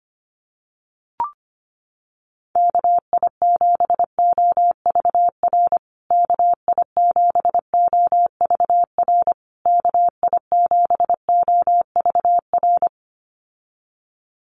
Morse code, heard as a keyed tone, about 25 wpm, 700 Hz. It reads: KI7OVR KI7OVR KI7OVR
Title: USA Call Signs Repeated 3x 25wpm